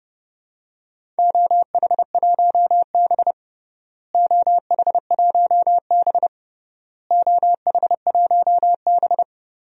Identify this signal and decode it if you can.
Morse code, heard as a keyed tone, about 32 wpm, 700 Hz. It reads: OH1B OH1B OH1B
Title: DXCC-100 Call Signs - 4 Characters - Repeated 3x - 30wpm